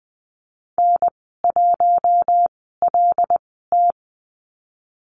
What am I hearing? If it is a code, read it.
Morse code, heard as a keyed tone, about 20 wpm, spaced as for 14 wpm, 700 Hz. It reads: N1LT
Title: 088 Lesson 15g - Number 5 - Call Signs 20wpm